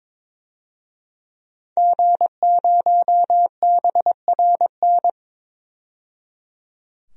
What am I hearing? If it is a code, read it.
Morse code, heard as a keyed tone, about 22 wpm, 700 Hz. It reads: G0BRN
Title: DXCC-100 Call Signs - 5 Characters - 22wpm